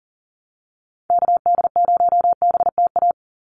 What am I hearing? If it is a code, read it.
Morse code, heard as a keyed tone, about 40 wpm, 700 Hz. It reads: KD0BTA
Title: USA Call Signs 40wpm